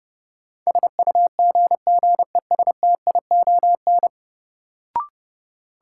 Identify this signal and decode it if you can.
Morse code, heard as a keyed tone, about 32 wpm, 700 Hz. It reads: SUGGESTION
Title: Sentences from Top 1600 Words - New Word Drill Alt - 30wpm